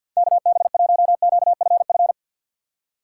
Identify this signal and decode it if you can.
Morse code, heard as a keyed tone, about 48 wpm, 700 Hz. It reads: KB1QFF
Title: USA Call Signs - 6 Characters - 6x Character Spacing - 50wpm